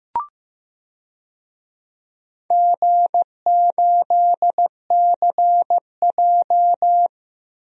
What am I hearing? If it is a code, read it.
Morse code, heard as a keyed tone, about 15 wpm, 700 Hz. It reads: G8CJ
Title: DXCC-100 Call Signs - 4 Characters - 15wpm